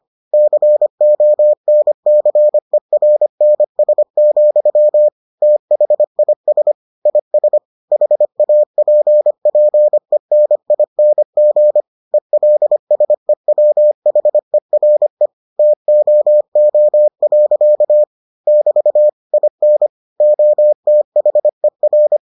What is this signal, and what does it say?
Morse code, heard as a keyed tone, about 25 wpm, 600 Hz. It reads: CONCERNS, THIS IS HAPPENING ELSEWHERE TOO. <BT> IN OTHER